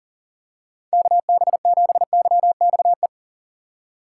Morse code, heard as a keyed tone, about 40 wpm, 700 Hz. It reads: KB7YXE
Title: USA Call Signs - 6 Characters - 3x Character Spacing - 40wpm